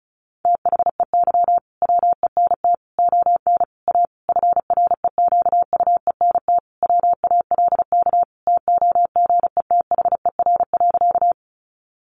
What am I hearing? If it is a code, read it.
Morse code, heard as a keyed tone, about 34 wpm, 700 Hz. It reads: THEY WENT ON A FREQUENT WALK TOGETHER.
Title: Sentences from Top 1400 Words 35wpm